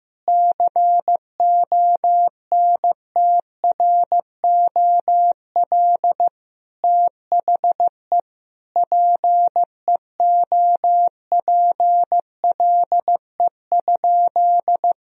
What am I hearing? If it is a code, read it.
Morse code, heard as a keyed tone, about 15 wpm, 700 Hz. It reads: CONTROL THE PEOPLE?
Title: Sentences from Top 1000 Words - Review Part 3 - 15wpm